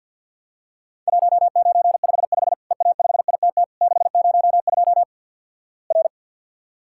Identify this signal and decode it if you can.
Morse code, heard as a keyed tone, about 53 wpm, 700 Hz. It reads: 1955 EA5ITT 602 R
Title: Worked All Europe DX Contest - QTC Exchange with Repeats - 50wpm